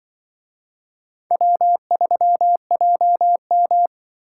Morse code, heard as a keyed tone, about 24 wpm, 700 Hz. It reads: W3JM
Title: Speed-Racing - Four Letter-Number Combinations - 20wpm